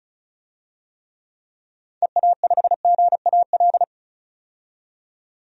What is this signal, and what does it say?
Morse code, heard as a keyed tone, about 36 wpm, 700 Hz. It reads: EA5GAL
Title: DXCC-100 Call Signs - 6 Characters - 35wpm